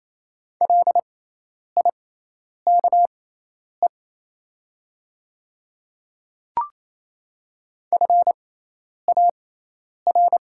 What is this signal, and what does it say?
Morse code, heard as a keyed tone, about 28 wpm, spaced as for 10 wpm, 700 Hz. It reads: LIKE FAR
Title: 6x Character Spacing - Top 300 Words - 28wpm